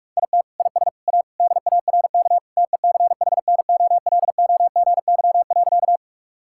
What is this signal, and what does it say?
Morse code, heard as a keyed tone, about 45 wpm, 700 Hz. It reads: IT IS A DARK TECHNOLOGY.
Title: Sentences from Top 1000 Words - Review Part 5 - 45wpm